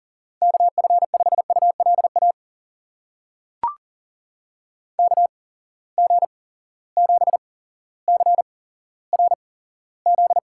Morse code, heard as a keyed tone, about 38 wpm, spaced as for 17 wpm, 700 Hz. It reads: KF5ULA KG7CRZ
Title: USA Call Signs - 6 Characters - 8x Character Spacing - 40wpm